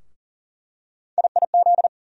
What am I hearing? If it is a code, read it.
Morse code, heard as a keyed tone, about 40 wpm, 700 Hz. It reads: IIZ
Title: USA Call Sign Postfixes - 3 Letters 40wpm